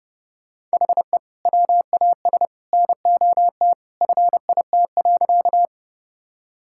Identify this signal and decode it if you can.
Morse code, heard as a keyed tone, about 30 wpm, 700 Hz. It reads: HE WAS NOT FIT.
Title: Sentences from Top 1000 Words - Review Part 5 - 30wpm